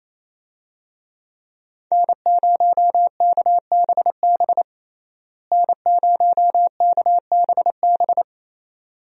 Morse code, heard as a keyed tone, about 28 wpm, 700 Hz. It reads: N0KBB N0KBB
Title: USA Call Signs - 5 Characters - Repeated 2x - 28wpm